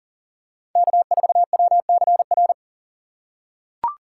Morse code, heard as a keyed tone, about 40 wpm, 700 Hz. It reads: K4WCR
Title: USA Call Signs - 5 Characters - 2x Character Spacing - 40wpm